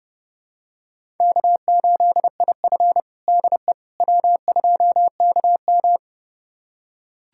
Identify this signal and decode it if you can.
Morse code, heard as a keyed tone, about 32 wpm, 700 Hz. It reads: K8IF DE W2KM
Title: USA Callsign DE Callsign - 4 Characters - 30wpm